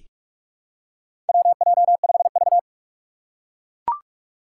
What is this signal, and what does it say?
Morse code, heard as a keyed tone, about 41 wpm, 700 Hz. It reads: WJ5V
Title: Four Challenging Letter-Number Combinations 45wpm